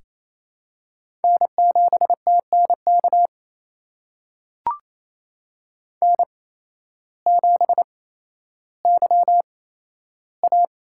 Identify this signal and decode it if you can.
Morse code, heard as a keyed tone, about 28 wpm, spaced as for 13 wpm, 700 Hz. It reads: N7TNK N7YA
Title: USA Call Signs - 5 Characters - 8x Character Spacing - 28wpm